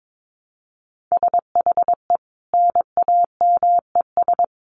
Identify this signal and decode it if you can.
Morse code, heard as a keyed tone, about 22 wpm, 700 Hz. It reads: SHE NAMES
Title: Easy Phrases 22wpm